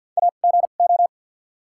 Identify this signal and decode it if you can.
Morse code, heard as a keyed tone, about 54 wpm, spaced as for 37 wpm, 700 Hz. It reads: AGO
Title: 2x Character Spacing - Top 500 Words - 50wpm